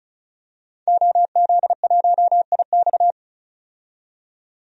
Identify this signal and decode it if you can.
Morse code, heard as a keyed tone, about 36 wpm, 700 Hz. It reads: OZ1IX
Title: DXCC-100 Call Signs - 5 Characters - 35wpm